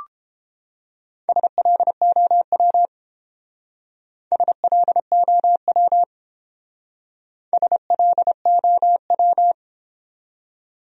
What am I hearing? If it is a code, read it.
Morse code, heard as a keyed tone, about 29 wpm, 700 Hz. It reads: SLOW SLOW SLOW